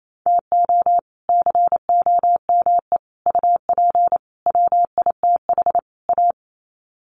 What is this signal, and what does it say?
Morse code, heard as a keyed tone, about 28 wpm, 700 Hz. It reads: TO COME UP WITH A
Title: Sets of 5 Words 28wpm